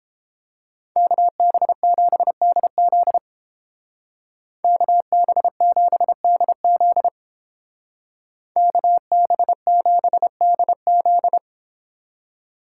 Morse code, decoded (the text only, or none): KB7DZ KB7DZ KB7DZ